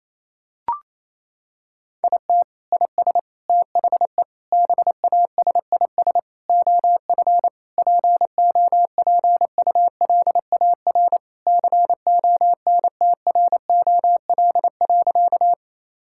Morse code, heard as a keyed tone, about 28 wpm, 700 Hz. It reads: IT IS THE BASIS OF POPULAR CONTROL.
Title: Sentences from Top 1000 Words - Review Part 5 - 28wpm